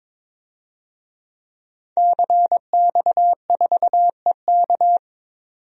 Morse code, heard as a keyed tone, about 21 wpm, 700 Hz. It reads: CX4EK
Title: DXCC-100 Call Signs - 5 Characters - 22wpm